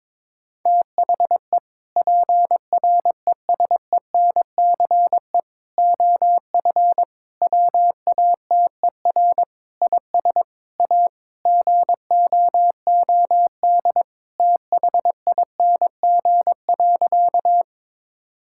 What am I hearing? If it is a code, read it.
Morse code, heard as a keyed tone, about 22 wpm, 700 Hz. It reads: THE PRESENCE OF WATER IS A GOOD THING.
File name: practice / 22wpm / Sentences with 3-Word Phrases - 22wpm